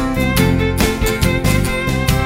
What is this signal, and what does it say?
There is no code.